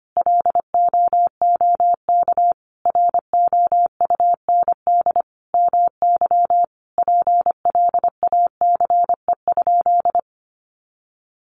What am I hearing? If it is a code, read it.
Morse code, heard as a keyed tone, about 25 wpm, 700 Hz. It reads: LOOK ROUND MY PLACE?
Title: Sentences from Top 200 Words 25wpm